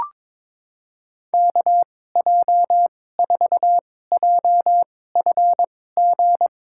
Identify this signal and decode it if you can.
Morse code, heard as a keyed tone, about 22 wpm, spaced as for 16 wpm, 700 Hz. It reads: KJ4JFG
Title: USA Call Signs - 6 Characters - 2x Character Spacing - 22wpm